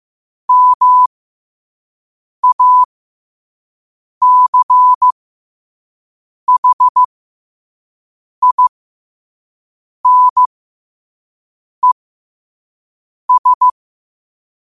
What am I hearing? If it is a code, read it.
Morse code, heard as a keyed tone, about 15 wpm, spaced as for 5 wpm, 1000 Hz. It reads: MACHINES
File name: qotdfiles / 5wpm